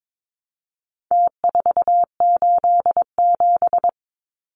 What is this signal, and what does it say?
Morse code, heard as a keyed tone, about 22 wpm, 700 Hz. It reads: T487
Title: Numbers - Contest Style - 4-Digit 22wpm